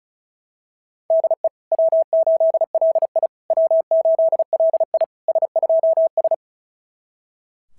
Morse code, heard as a keyed tone, about 36 wpm, 690 Hz. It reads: DE W8LI W8LI S2S
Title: SOTA QSO Practice 35wpm